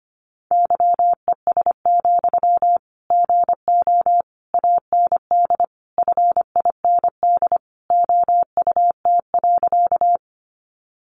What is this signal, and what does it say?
Morse code, heard as a keyed tone, about 25 wpm, 700 Hz. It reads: YES, GO AND FIND OUT.
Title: Sentences from Top 500 Words 25wpm